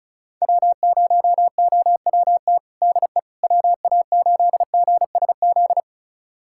Morse code, heard as a keyed tone, about 34 wpm, 700 Hz. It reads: W0OWT DE WA8GSZ
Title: USA Callsign DE Callsign 35wpm